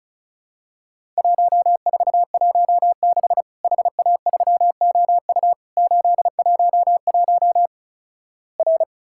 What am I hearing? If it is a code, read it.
Morse code, heard as a keyed tone, about 36 wpm, 700 Hz. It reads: 1416 HA3OU 811 R
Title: Worked All Europe DX Contest - QTC Exchange - 35wpm